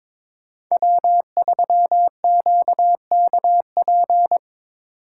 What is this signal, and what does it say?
Morse code, heard as a keyed tone, about 22 wpm, 700 Hz. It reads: W3QKP
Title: USA Call Signs - 8x Character Spacing - 22wpm